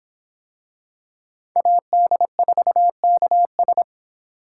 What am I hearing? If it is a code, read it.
Morse code, heard as a keyed tone, about 22 wpm, 700 Hz. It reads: AD4KS